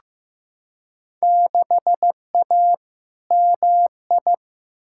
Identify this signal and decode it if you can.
Morse code, heard as a keyed tone, about 15 wpm, 700 Hz. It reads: 6A MI
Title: Field Day Exchanges - 15wpm